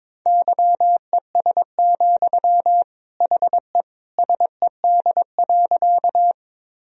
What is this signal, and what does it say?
Morse code, heard as a keyed tone, about 22 wpm, 700 Hz. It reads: YES, HE SED.
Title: Sentences from Top 500 Words and Abbreviations 22wpm